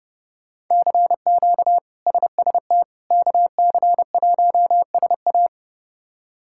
Morse code, heard as a keyed tone, about 30 wpm, 700 Hz. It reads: CQ SST KC1SA
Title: K1USN SST QSO Practice - 30wpm